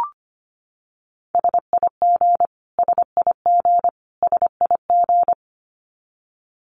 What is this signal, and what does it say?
Morse code, heard as a keyed tone, about 25 wpm, 700 Hz. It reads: SIG SIG SIG
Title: Abbreviations Repeated 3x - 25wpm